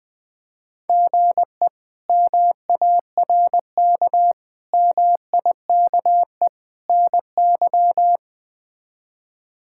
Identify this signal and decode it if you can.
Morse code, heard as a keyed tone, about 20 wpm, 700 Hz. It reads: GE MARK MIKE NY
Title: K1USN SST QSO Practice - 20wpm